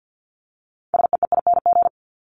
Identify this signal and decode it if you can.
Morse code, heard as a keyed tone, about 50 wpm, 700 Hz. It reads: SEEING
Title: ING Suffix - Top 100 Words with Base Words 50wpm